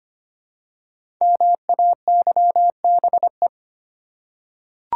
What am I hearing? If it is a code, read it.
Morse code, heard as a keyed tone, about 25 wpm, 700 Hz. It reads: MAYBE
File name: 5 Letter Words - First 3 Letters Given - 25wpm